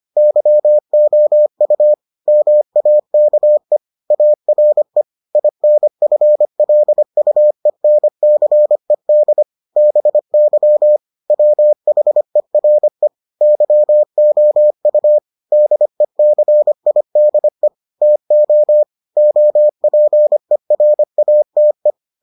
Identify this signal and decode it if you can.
Morse code, heard as a keyed tone, about 25 wpm, 600 Hz. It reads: YOU MAKE ARE INFLUENCED BY WHERE YOU DECIDE TO OPERATE